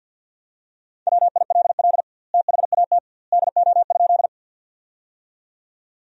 Morse code, heard as a keyed tone, about 50 wpm, 700 Hz. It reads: WILL THAT DO?
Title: 2x Word-Spacing - Sentences from Top 100 Words - 50wpm